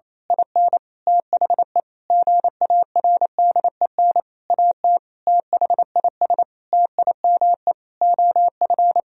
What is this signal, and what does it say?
Morse code, heard as a keyed tone, about 28 wpm, 700 Hz. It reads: IN THE GARDEN AT THIS TIME OF
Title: Sentences from Top 1000 Words - Review Part 4 - 28wpm